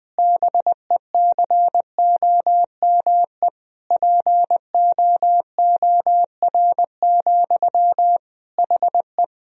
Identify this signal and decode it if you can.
Morse code, heard as a keyed tone, about 20 wpm, 700 Hz. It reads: BECOME POOR, HE
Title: Sentences from Top 1000 Words - Review Part 3 - 20wpm